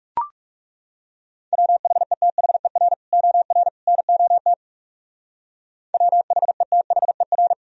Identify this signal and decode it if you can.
Morse code, heard as a keyed tone, about 40 wpm, 700 Hz. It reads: WHETHER OR NOT WHETHER